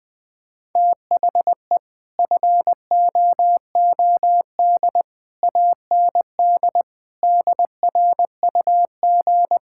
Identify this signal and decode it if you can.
Morse code, heard as a keyed tone, about 20 wpm, 700 Hz. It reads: THE FOOD AND DRUG
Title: Build A Phrase - 5-Word Phrases - 20wpm